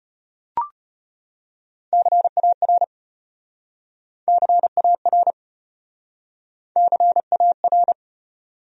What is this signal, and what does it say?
Morse code, heard as a keyed tone, about 33 wpm, 700 Hz. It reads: CAR CAR CAR